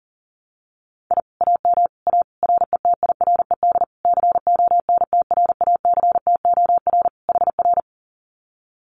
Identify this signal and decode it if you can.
Morse code, heard as a keyed tone, about 40 wpm, 700 Hz. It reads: I AM A RETIRED CONTRACTOR HR
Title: Common QSO Phrases Encore 40wpm